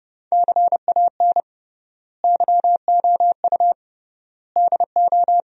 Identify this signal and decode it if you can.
Morse code, heard as a keyed tone, about 31 wpm, 700 Hz. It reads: CAN YOU DO
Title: 3x Word-Spacing - Sentences from Top 100 Words - 30wpm